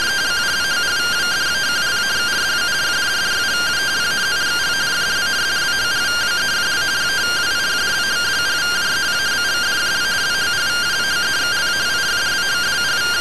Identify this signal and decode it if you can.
Two-tone teleprinter signal, about 170 Hz shift.